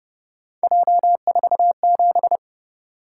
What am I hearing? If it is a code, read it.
Morse code, heard as a keyed tone, about 28 wpm, 700 Hz. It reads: J47